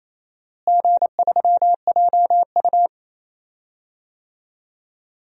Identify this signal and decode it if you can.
Morse code, heard as a keyed tone, about 28 wpm, 700 Hz. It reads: G3JU
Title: DXCC-100 Call Signs - 4 Characters - Repeated 3x - 28wpm